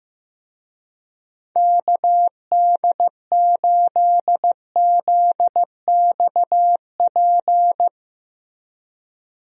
Morse code, heard as a keyed tone, about 15 wpm, 700 Hz. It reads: KD8ZXP
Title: USA Call Signs - 6 Characters - Repeated 3x - 15wpm